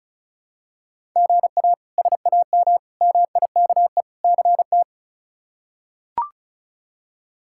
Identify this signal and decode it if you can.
Morse code, heard as a keyed tone, about 34 wpm, 700 Hz. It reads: GA SAM MIKE CT
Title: K1USN SST QSO Practice - 35wpm